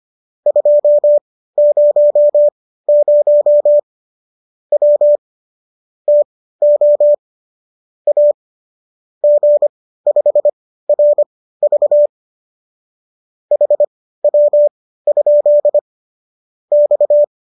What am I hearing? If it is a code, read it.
Morse code, heard as a keyed tone, about 25 wpm, spaced as for 15 wpm, 590 Hz. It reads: 200 W TO A G5RV HW? X
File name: qso204 15